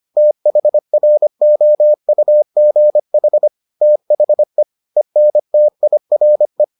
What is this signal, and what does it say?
Morse code, heard as a keyed tone, about 25 wpm, 600 Hz. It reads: THROUGH THE ENTIRE